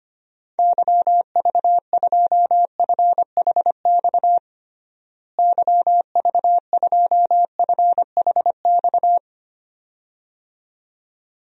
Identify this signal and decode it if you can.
Morse code, heard as a keyed tone, about 25 wpm, 700 Hz. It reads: YV2FHX YV2FHX
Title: DXCC-100 Call Signs - 6 Characters - Repeated 2x - 25wpm